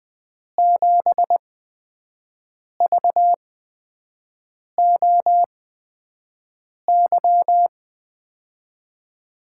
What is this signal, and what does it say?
Morse code, heard as a keyed tone, about 21 wpm, spaced as for 5 wpm, 700 Hz. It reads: 7VOY
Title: USA Call Signs - 6 Characters - 8x Character Spacing - 20wpm